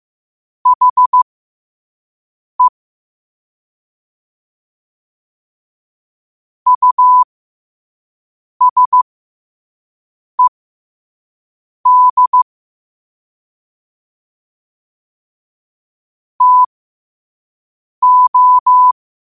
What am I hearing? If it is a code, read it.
Morse code, heard as a keyed tone, about 15 wpm, spaced as for 5 wpm, 1000 Hz. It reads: HE USED TO
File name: qotdfiles / 5wpm